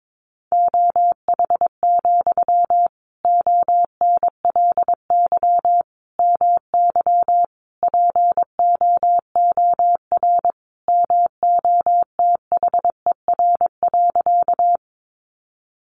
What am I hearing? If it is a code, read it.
Morse code, heard as a keyed tone, about 22 wpm, 700 Hz. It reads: OH, ONLY MY POOR MOTHER.
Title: Sentences from Top 1000 Words 22wpm